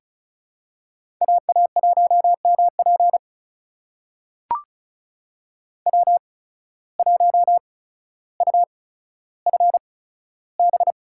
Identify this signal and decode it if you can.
Morse code, heard as a keyed tone, about 34 wpm, spaced as for 15 wpm, 700 Hz. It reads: AA1MP W1UFB
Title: USA Call Signs - 5 Characters - 8x Character Spacing - 35wpm